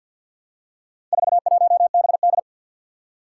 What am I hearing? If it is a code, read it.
Morse code, heard as a keyed tone, about 53 wpm, 700 Hz. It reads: 41BD
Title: Four Challenging Letter-Number Combinations 50wpm